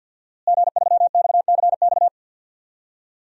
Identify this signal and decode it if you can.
Morse code, heard as a keyed tone, about 51 wpm, 700 Hz. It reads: G3XCX
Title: DXCC-100 Call Signs - 5 Characters - Repeated 3x - 50wpm